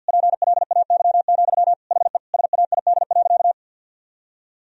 Morse code, heard as a keyed tone, about 50 wpm, 700 Hz. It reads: PLAY, HE SAID.
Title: Sentences from Top 1000 Words - Review Part 4 - 50wpm